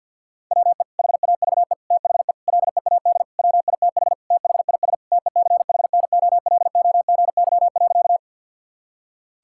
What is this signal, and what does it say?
Morse code, heard as a keyed tone, about 50 wpm, 700 Hz. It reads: WE HAVE THE LEAD WITH THIS TECHNOLOGY.
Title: Sentences from Top 1000 Words - Review Part 5 - 50wpm